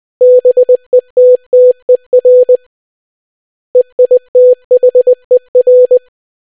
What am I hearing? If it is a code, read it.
Morse code, heard as a keyed tone, about 20 wpm, 500 Hz. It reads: BETTER EITHER